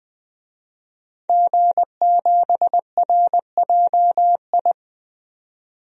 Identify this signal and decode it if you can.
Morse code, heard as a keyed tone, about 20 wpm, 700 Hz. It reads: G7RJI
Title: DXCC-100 Call Signs - 5 Characters - 20wpm